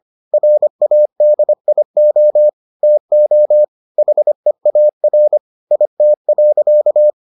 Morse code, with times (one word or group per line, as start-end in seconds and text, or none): 0.3-2.5 RADIO
2.8-3.6 TO
4.0-5.4 HEAR
5.7-7.1 IT.